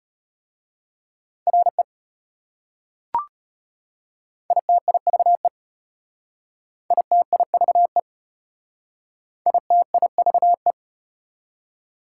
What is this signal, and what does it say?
Morse code, heard as a keyed tone, about 34 wpm, 700 Hz. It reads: RE ITIVE ITIVE ITIVE